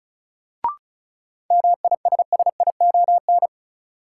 Morse code, heard as a keyed tone, about 36 wpm, 700 Hz. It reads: MISSION
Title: Sentences from Top 1600 Words - New Word Drill Alt - 35wpm